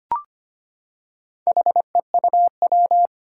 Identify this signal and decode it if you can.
Morse code, heard as a keyed tone, about 25 wpm, 700 Hz. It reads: HEUW
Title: Four Challenging Letter-Number Combinations 25wpm